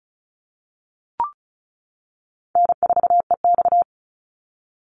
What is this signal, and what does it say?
Morse code, heard as a keyed tone, about 34 wpm, 700 Hz. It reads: N4EX
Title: NA SOTA Activator Callsigns 35wpm